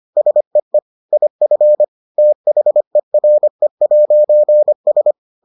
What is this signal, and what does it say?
Morse code, heard as a keyed tone, about 25 wpm, 600 Hz. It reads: SEE IF THERE'S